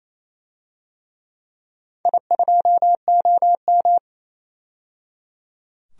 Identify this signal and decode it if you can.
Morse code, heard as a keyed tone, about 28 wpm, 700 Hz. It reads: I2OM
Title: DXCC-100 Call Signs - 4 Characters - 28wpm